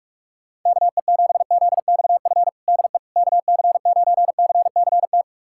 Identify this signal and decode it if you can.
Morse code, heard as a keyed tone, about 47 wpm, 700 Hz. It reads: KE7ZXF DE KC9CCT
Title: USA Callsign DE Callsign - 6 Characters - 45wpm